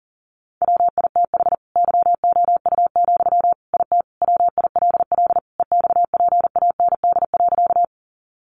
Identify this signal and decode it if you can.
Morse code, heard as a keyed tone, about 40 wpm, 700 Hz. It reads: WITH YOU, IT WILL EXPAND.
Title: Sentences from Top 1400 Words 40wpm